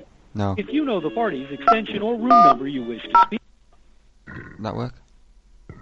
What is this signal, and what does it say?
DTMF, heard as telephone touch tones, tones of about 100 ms, with speech. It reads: 210